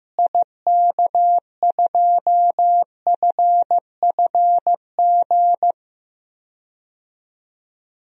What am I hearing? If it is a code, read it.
Morse code, heard as a keyed tone, about 15 wpm, 700 Hz. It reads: IK2FFG
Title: DXCC-100 Call Signs - 6 Characters - Repeated 2x - 15wpm